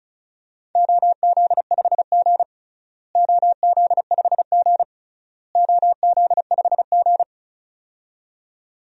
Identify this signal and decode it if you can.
Morse code, heard as a keyed tone, about 34 wpm, 700 Hz. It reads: OZ5G OZ5G OZ5G
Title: DXCC-100 Call Signs - 4 Characters - Repeated 3x - 35wpm